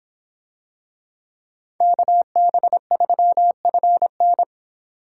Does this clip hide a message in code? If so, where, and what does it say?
Yes: Morse code from 1.8 to 4.4 s, heard as a keyed tone, about 24 wpm, 700 Hz. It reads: KB3FN